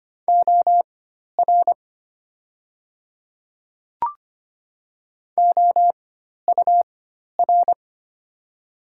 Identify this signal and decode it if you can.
Morse code, heard as a keyed tone, about 25 wpm, spaced as for 12 wpm, 700 Hz. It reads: OR OUR
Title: 4x Character Spacing - Top 200 Words - 25wpm